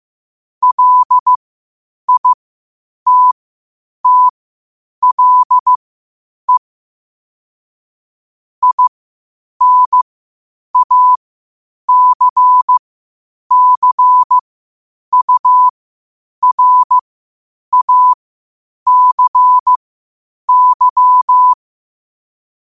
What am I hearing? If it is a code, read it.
Morse code, heard as a keyed tone, about 15 wpm, spaced as for 8 wpm, 1000 Hz. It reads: LITTLE INACCURACY